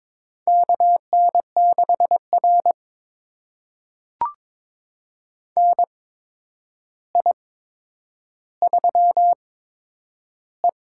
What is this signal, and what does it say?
Morse code, heard as a keyed tone, about 22 wpm, spaced as for 9 wpm, 700 Hz. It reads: KN6R NI3E
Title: USA Call Signs - 4 Characters - 8x Character Spacing - 22wpm